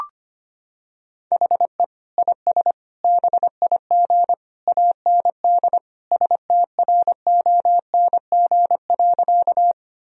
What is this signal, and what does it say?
Morse code, heard as a keyed tone, about 25 wpm, 700 Hz. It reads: HE IS BIG AND STRONG.
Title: Sentences from Top 1000 Words - Review Part 3 - 25wpm